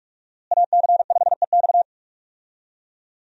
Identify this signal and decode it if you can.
Morse code, heard as a keyed tone, about 46 wpm, 700 Hz. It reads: AC5EX
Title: USA Call Signs - 5 Characters - Repeated 2x - 45wpm